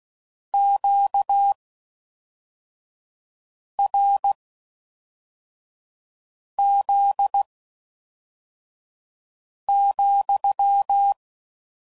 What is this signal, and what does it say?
Morse code, heard as a keyed tone, about 16 wpm, spaced as for 4 wpm, 800 Hz. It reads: QRZ,